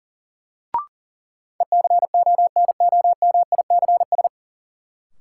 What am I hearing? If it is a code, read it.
Morse code, heard as a keyed tone, about 38 wpm, 700 Hz. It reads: ECONOMICS
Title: Sentences from Top 1000 Words - Review Part 3 - New Word Drill - 40wpm